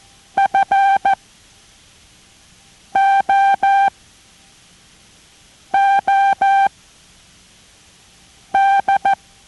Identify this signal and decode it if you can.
Morse code, heard as a keyed tone, about 14 wpm, spaced as for 4 wpm, 790 Hz. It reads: FOOD